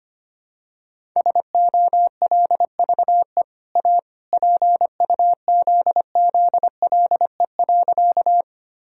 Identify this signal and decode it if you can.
Morse code, heard as a keyed tone, about 25 wpm, 700 Hz. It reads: SOLVE A PUZZLE.
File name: Easy Sentences with Hints - 25wpm